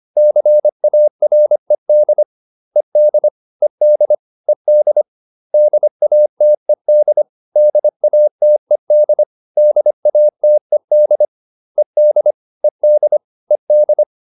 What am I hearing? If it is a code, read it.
Morse code, heard as a keyed tone, about 25 wpm, 600 Hz. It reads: CARED ED ED ED DATED DATED DATED ED ED ED